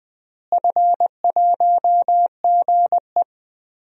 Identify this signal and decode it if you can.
Morse code, heard as a keyed tone, about 20 wpm, 700 Hz. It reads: F1GE
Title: DXCC-100 Call Signs - 4 Characters - Repeated 3x - 20wpm